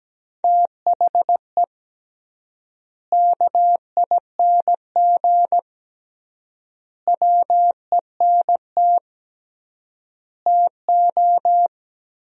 Morse code, heard as a keyed tone, about 17 wpm, 700 Hz. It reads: THE KING WENT TO
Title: 3x Word-Spacing - Sentences from Top 400 Words - 17wpm